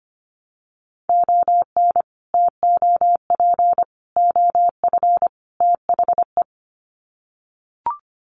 Morse code, heard as a keyed tone, about 25 wpm, 700 Hz. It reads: ON TOP OF THE
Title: Sets of 4 Words 25wpm